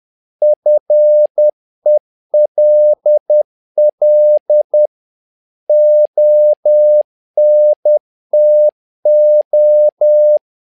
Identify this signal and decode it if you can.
Morse code, heard as a keyed tone, about 10 wpm, 600 Hz. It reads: FELL ONTO
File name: learning center / tower trouble / tower trouble 10wpm